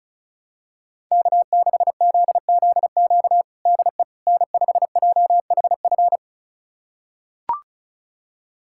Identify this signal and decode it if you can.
Morse code, heard as a keyed tone, about 36 wpm, 700 Hz. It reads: K6ZZQ DE N5JHF
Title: USA Callsign DE Callsign - 5 Characters - 35wpm